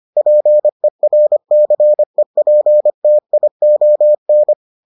Morse code, heard as a keyed tone, about 25 wpm, 600 Hz. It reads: PERCEPTION